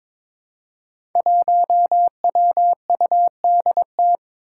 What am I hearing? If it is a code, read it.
Morse code, heard as a keyed tone, about 22 wpm, 700 Hz. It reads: 1WUDT
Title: Five Challenging Letter-Number Combinations 22wpm